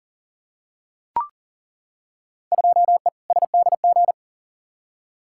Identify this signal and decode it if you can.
Morse code, heard as a keyed tone, about 38 wpm, 700 Hz. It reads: 2E SDG
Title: Field Day Exchanges - 40wpm